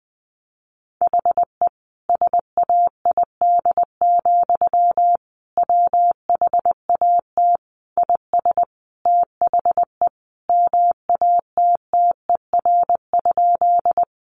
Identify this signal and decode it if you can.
Morse code, heard as a keyed tone, about 20 wpm, 700 Hz. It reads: HE SAID, WHAT IS THE MATTER?
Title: Sentences from Top 700 Words 20wpm